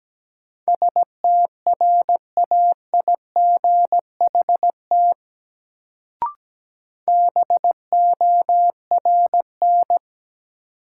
Morse code, heard as a keyed tone, about 17 wpm, 700 Hz. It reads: STRAIGHT BORN
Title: Sentences from Top 1000 Words - New Word Drill - 17wpm